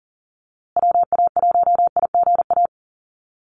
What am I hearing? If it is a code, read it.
Morse code, heard as a keyed tone, about 40 wpm, 700 Hz. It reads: WA1IGA
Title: USA Call Signs Repeated 3x 40wpm